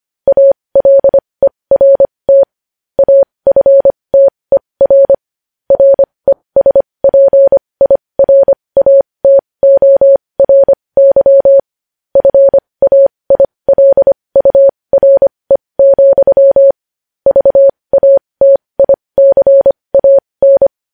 Morse code, heard as a keyed tone, about 25 wpm, spaced as for 20 wpm, 550 Hz. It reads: ALERT AFTER RESPIRATORY FAILURE, VATICAN
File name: podcast / News headlines-1x-25x20